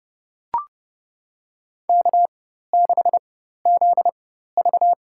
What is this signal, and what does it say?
Morse code, heard as a keyed tone, about 30 wpm, spaced as for 14 wpm, 700 Hz. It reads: K6ZV
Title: USA Call Signs - 4 Characters - 4x Character Spacing - 30wpm